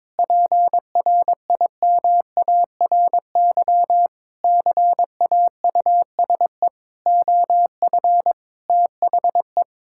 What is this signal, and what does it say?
Morse code, heard as a keyed tone, about 22 wpm, 700 Hz. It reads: PRIMARY CAUSE OF THE
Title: Sentences from Top 1000 Words - Review Part 5 - 22wpm